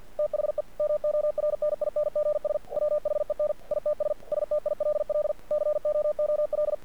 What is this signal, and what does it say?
Morse code, heard as a keyed tone, about 48 wpm, 600 Hz. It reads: THE MORNINGS WHEN ITS STILL COOL
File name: archived / CRSnet